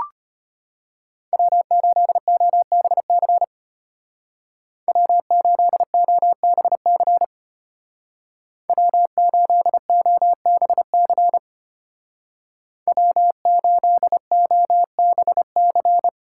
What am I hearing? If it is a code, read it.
Morse code, heard as a keyed tone, about 31 wpm, 700 Hz. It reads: W8OBC W8OBC W8OBC W8OBC
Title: Speed Racing - USA Call Signs 25wpm